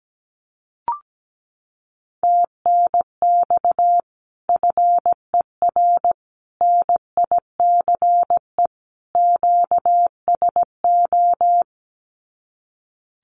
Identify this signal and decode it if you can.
Morse code, heard as a keyed tone, about 17 wpm, 700 Hz. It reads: TNX FER NICE QSO
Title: Common QSO Phrases 17wpm